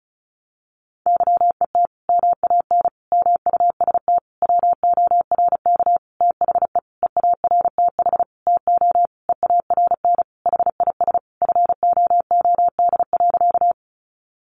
Morse code, heard as a keyed tone, about 34 wpm, 700 Hz. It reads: YET MAN MUST WORK THE EARTH TO EARN HIS FOOD.
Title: Sentences from Top 1500 Words 35wpm